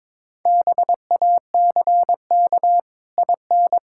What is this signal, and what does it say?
Morse code, heard as a keyed tone, about 22 wpm, 700 Hz. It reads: BACK IN
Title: Pangram Challenge v2 - 22wpm